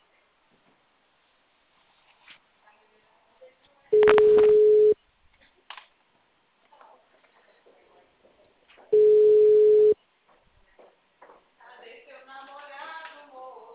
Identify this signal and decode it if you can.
Ringback tone (Europe, 425 Hz), with speech.
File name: out- -2015- - - .150